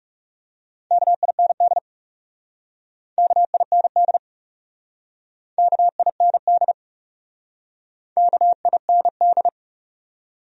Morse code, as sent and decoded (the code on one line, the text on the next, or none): -.- .. -. -.. / -.- .. -. -.. / -.- .. -. -.. / -.- .. -. -..
KIND KIND KIND KIND